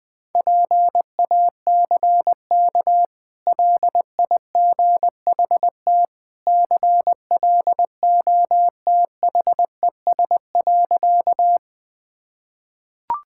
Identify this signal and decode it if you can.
Morse code, heard as a keyed tone, about 21 wpm, 700 Hz. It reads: PACK LIGHT CLOTHES.
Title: Easy Sentences with Hints - 20wpm